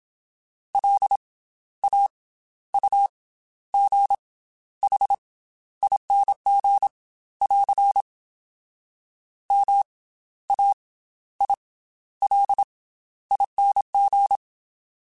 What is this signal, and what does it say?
Morse code, heard as a keyed tone, about 27 wpm, 800 Hz. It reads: L A U G H ING <AR> M A I L ING
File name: Special INGs / Special INGs-mp3s / Slow ing @25